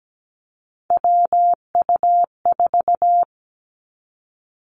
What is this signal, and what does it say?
Morse code, heard as a keyed tone, about 17 wpm, 700 Hz. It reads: WU4
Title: Three Challenging Letter-Number Combinations 17wpm